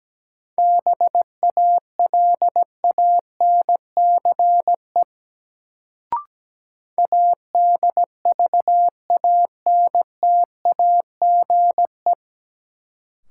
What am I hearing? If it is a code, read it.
Morse code, heard as a keyed tone, about 17 wpm, 700 Hz. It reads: BALANCE ADVANTAGE
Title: Sentences from Top 900 Words - New Word Drill - 17wpm